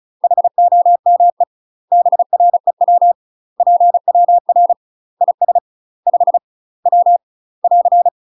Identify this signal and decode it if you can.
Morse code, heard as a keyed tone, about 34 wpm, 700 Hz. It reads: HOME BREW PWR IS 5 W <AR>
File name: UK MASTER QSOs TRAINING / 35WPM / UK MASTER QSOs 0041